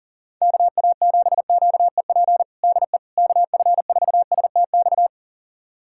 Morse code, heard as a keyed tone, about 37 wpm, 700 Hz. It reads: KA7QEP DE KF4STX
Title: USA Callsign DE Callsign 40wpm